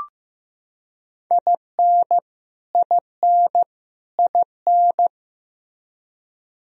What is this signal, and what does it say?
Morse code, heard as a keyed tone, about 15 wpm, 700 Hz. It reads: IN IN IN
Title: Top 100 Words Repeated 15wpm